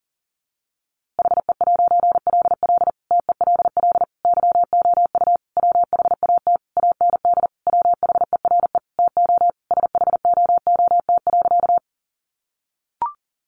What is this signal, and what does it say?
Morse code, heard as a keyed tone, about 40 wpm, 700 Hz. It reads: HE'LL TELL YOU WHAT AND WHERE TO SHOOT.
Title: A Sound Of Thunder 40wpm